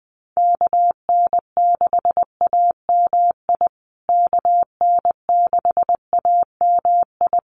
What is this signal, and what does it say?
Morse code, heard as a keyed tone, about 19 wpm, 700 Hz. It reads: KN6AMI KN6AMI
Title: USA Call Signs Repeated 3x 20wpm